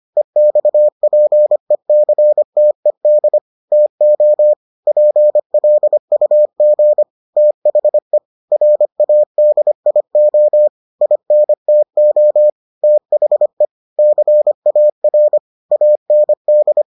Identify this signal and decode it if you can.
Morse code, heard as a keyed tone, about 25 wpm, 600 Hz. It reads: EXPECTED TO PLUG THE RADIO INTO THE CAR AND